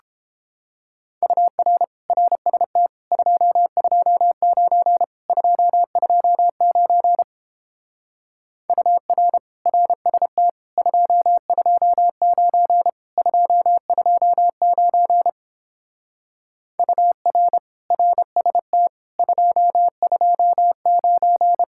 Morse code, heard as a keyed tone, about 30 wpm, 700 Hz. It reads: UR RST 229 229 UR RST 229 229 UR RST 229